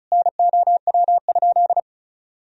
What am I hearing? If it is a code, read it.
Morse code, heard as a keyed tone, about 36 wpm, 700 Hz. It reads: NOW?
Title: Sentences from Top 400 Words and Abbreviations 35wpm